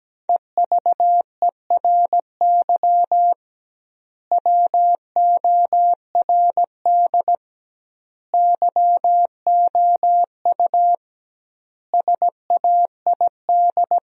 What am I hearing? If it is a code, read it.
Morse code, heard as a keyed tone, about 17 wpm, 700 Hz. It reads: EVERY WORD YOU SAID
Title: 2x Word-Spacing - Sentences from Top 400 Words - 17wpm